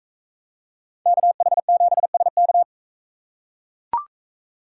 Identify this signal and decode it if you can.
Morse code, heard as a keyed tone, about 43 wpm, 700 Hz. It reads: KH7SK